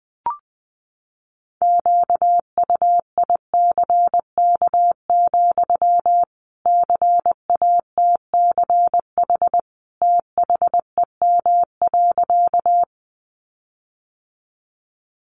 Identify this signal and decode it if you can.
Morse code, heard as a keyed tone, about 20 wpm, 700 Hz. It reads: QUICK, CATCH THEM.
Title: Sentences from Top 1500 Words 20wpm